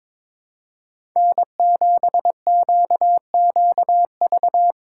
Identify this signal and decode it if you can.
Morse code, heard as a keyed tone, about 23 wpm, 700 Hz. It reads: N7QQV
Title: USA Call Signs - 5 Characters - 2x Character Spacing - 22wpm